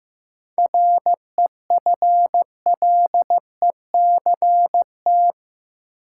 Morse code, heard as a keyed tone, about 15 wpm, 700 Hz. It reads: REFLECT
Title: Sentences from Top 1500 Words - New Word Drill - 15wpm